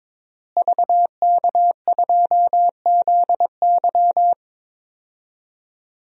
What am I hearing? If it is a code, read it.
Morse code, heard as a keyed tone, about 22 wpm, 700 Hz. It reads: VK2ZY
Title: DXCC-100 Call Signs - 5 Characters - Repeated 2x - 22wpm